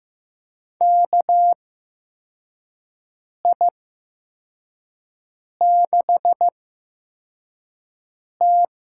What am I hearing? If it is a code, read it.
Morse code, heard as a keyed tone, about 15 wpm, spaced as for 4 wpm, 700 Hz. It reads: KI6T
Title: USA Call Signs - 8x Character Spacing - 15wpm